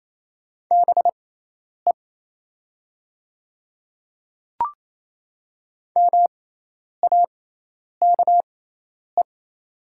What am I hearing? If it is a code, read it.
Morse code, heard as a keyed tone, about 28 wpm, spaced as for 10 wpm, 700 Hz. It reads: BE MAKE
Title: 6x Character Spacing - Top 300 Words - 28wpm